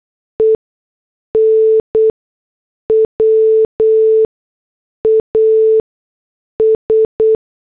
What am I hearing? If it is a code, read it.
Morse code, heard as a keyed tone, about 8 wpm, spaced as for 6 wpm, 440 Hz. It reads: EN WAS
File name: morse / headlines